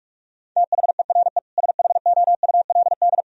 Morse code, heard as a keyed tone, about 41 wpm, 700 Hz. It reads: THERE SHOULD
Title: Sentences from Top 1000 Words - Review Part 3 - 45wpm